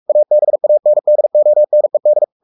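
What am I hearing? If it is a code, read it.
Morse code, heard as a keyed tone, about 44 wpm, 600 Hz. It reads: ABANDONED
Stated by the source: Morse code at 43 wpm sending Ken Follett's novel Never in English, from CW med Gnister.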